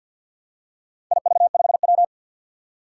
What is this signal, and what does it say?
Morse code, heard as a keyed tone, about 53 wpm, 700 Hz. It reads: IV5W
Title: Four Challenging Letter-Number Combinations 50wpm